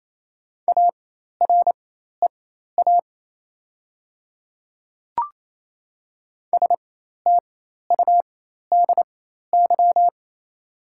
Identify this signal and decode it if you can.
Morse code, heard as a keyed tone, about 28 wpm, spaced as for 13 wpm, 700 Hz. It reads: AREA STUDY
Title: 4x Character Spacing - Top 500 Words - 28wpm